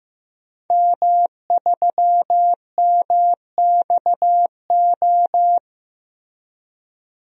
Morse code, heard as a keyed tone, about 16 wpm, 700 Hz. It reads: M3MXO
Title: DXCC-100 Call Signs - 5 Characters - Repeated 2x - 15wpm